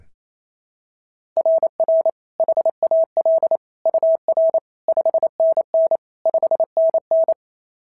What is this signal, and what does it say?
Morse code, heard as a keyed tone, about 27 wpm, 690 Hz. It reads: RR HAL UR 5NN 5NN
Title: SOTA QSO Practice 28wpm